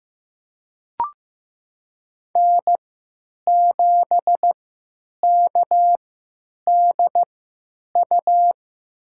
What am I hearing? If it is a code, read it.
Morse code, heard as a keyed tone, about 15 wpm, spaced as for 9 wpm, 700 Hz. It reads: N7KDU
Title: USA Call Signs - 5 Characters - 3x Character Spacing - 15wpm